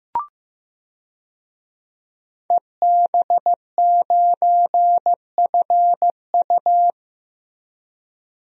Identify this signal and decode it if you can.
Morse code, heard as a keyed tone, about 16 wpm, 700 Hz. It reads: EB9FU
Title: DXCC-100 Call Signs - 5 Characters - Repeated 3x - 15wpm